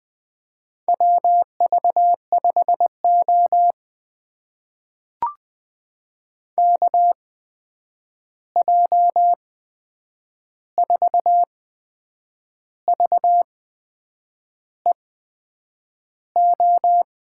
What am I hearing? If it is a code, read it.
Morse code, heard as a keyed tone, about 20 wpm, spaced as for 8 wpm, 700 Hz. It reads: WV5O KJ4VEO
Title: USA Call Signs - 8x Character Spacing - 20wpm